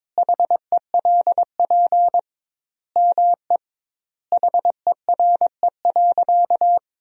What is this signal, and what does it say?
Morse code, heard as a keyed tone, about 22 wpm, 700 Hz. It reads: HELP ME HERE.
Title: 2x Word-Spacing - Sentences from Top 200 Words - 22wpm